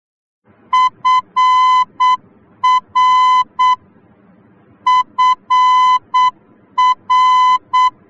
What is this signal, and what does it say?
Morse code, heard as a keyed tone, about 8 wpm, 1010 Hz. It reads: FR FR
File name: germany / mx / FR-D